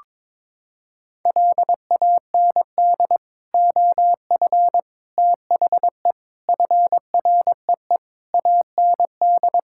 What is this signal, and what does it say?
Morse code, heard as a keyed tone, about 21 wpm, 700 Hz. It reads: LAND OF THE FREE AND
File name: Sets of 9 Words - 22wpm